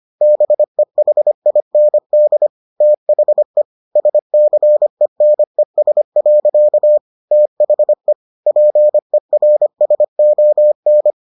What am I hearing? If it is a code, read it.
Morse code, heard as a keyed tone, about 25 wpm, 600 Hz. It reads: BEHIND THE SCENES. THE PERSON